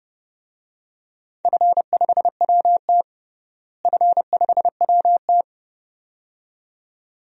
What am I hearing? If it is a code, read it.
Morse code, heard as a keyed tone, about 30 wpm, 700 Hz. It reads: F5WT F5WT
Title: DXCC-100 Call Signs - 4 Characters - Repeated 2x - 30wpm